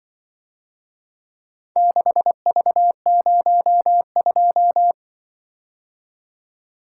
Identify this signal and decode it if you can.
Morse code, heard as a keyed tone, about 24 wpm, 700 Hz. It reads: 6V02
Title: Speed-Racing - Four Letter-Number Combinations - 20wpm